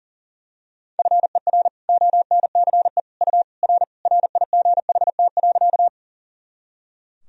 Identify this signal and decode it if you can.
Morse code, heard as a keyed tone, about 37 wpm, 700 Hz. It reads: FER ONCE U R RIGHT.
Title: Sentences from Top 300 Words and Abbreviations 40wpm